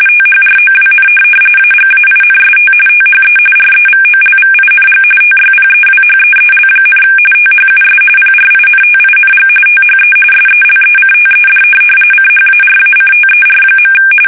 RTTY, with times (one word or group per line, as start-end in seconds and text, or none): 0.1-1.1 THROWN
1.6-2.7 AGAINST
2.9-3.4 THE
3.6-5.0 REVOLVING
5.2-7.0 TYPEWHEEL,
7.3-8.0 THUS
8.2-9.5 PRINTING
9.7-10.1 THE
10.3-11.6 SELECTED
12.1-13.9 CHARACTER.